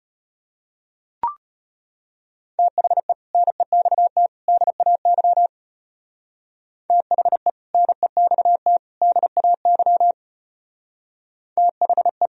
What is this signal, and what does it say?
Morse code, heard as a keyed tone, about 30 wpm, 700 Hz. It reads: THE NEXT DAY THE NEXT DAY THE
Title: Speed-Racing - Sets of 3 Words 25wpm